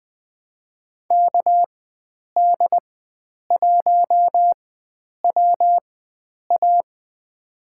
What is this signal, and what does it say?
Morse code, heard as a keyed tone, about 20 wpm, spaced as for 9 wpm, 700 Hz. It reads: KD1WA
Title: USA Call Signs - 4x Character Spacing - 20wpm